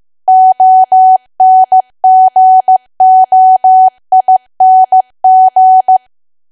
Morse code, heard as a keyed tone, about 15 wpm, 750 Hz. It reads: ONGOING
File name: cwa / W1AW-Level2 / 15WPM